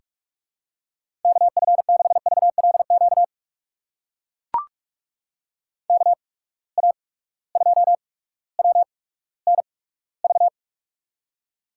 Morse code, heard as a keyed tone, about 43 wpm, spaced as for 19 wpm, 700 Hz. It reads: KF6VLQ KA2WNV
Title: USA Call Signs - 6 Characters - 8x Character Spacing - 45wpm